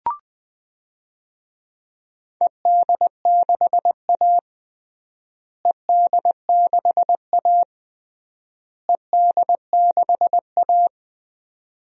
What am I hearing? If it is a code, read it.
Morse code, heard as a keyed tone, about 20 wpm, 700 Hz. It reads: ED6A ED6A ED6A
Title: DXCC-100 Call Signs - 4 Characters - Repeated 3x - 20wpm